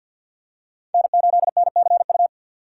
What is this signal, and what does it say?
Morse code, heard as a keyed tone, about 52 wpm, 700 Hz. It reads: N8NCU
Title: USA Call Signs - 5 Characters - 2x Character Spacing - 50wpm